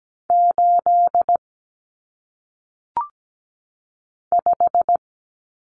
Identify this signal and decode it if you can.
Morse code, heard as a keyed tone, about 17 wpm, 700 Hz. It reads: 8 5
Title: Numbers - 1-Digit 17wpm